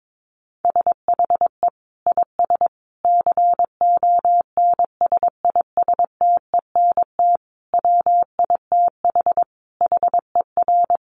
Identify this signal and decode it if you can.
Morse code, heard as a keyed tone, about 22 wpm, 700 Hz. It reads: SHE IS CONSISTENT WITH HER
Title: Sentences from Top 1500 Words 22wpm